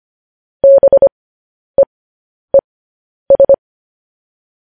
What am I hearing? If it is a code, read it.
Morse code, heard as a keyed tone, about 25 wpm, spaced as for 10 wpm, 550 Hz. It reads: BEES